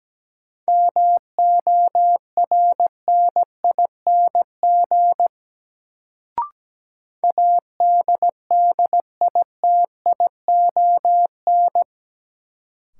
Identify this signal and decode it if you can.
Morse code, heard as a keyed tone, about 17 wpm, 700 Hz. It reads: MORNING ADDITION
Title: Sets of 3 Words - New Word Drill - 17wpm